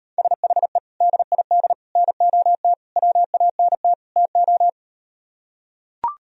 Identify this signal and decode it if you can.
Morse code, heard as a keyed tone, about 39 wpm, 700 Hz. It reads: SHE DID NOT WANT TO